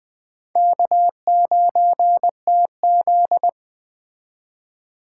Morse code, heard as a keyed tone, about 20 wpm, 700 Hz. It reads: K9TZ
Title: Speed Racing - USA 1x2 Call Signs 17wpm